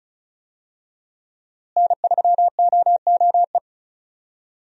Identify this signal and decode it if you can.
Morse code, heard as a keyed tone, about 34 wpm, 700 Hz. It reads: N3OOE
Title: USA Call Signs - 5 Characters - Repeated 2x - 35wpm